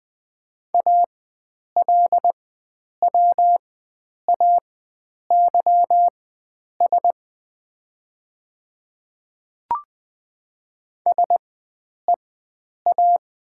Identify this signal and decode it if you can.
Morse code, heard as a keyed tone, about 20 wpm, spaced as for 9 wpm, 700 Hz. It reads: ALWAYS SEA
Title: 4x Character Spacing - Top 300 Words - 20wpm